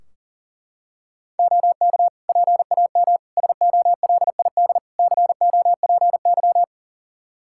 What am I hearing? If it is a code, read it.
Morse code, heard as a keyed tone, about 40 wpm, 700 Hz. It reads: OK PAM SOLID COPY
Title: Practice QSO - Exchange 3 - More Info - 40wpm